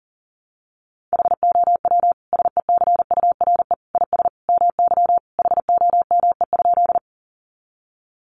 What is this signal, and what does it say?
Morse code, heard as a keyed tone, about 40 wpm, 700 Hz. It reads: HOW SECURE IS MY HOME?
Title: Sentences from Top 1200 Words 40wpm